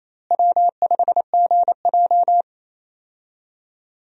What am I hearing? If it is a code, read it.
Morse code, heard as a keyed tone, about 27 wpm, 700 Hz. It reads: W5GJ
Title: USA Call Signs - 4 Characters - Repeated 3x - 28wpm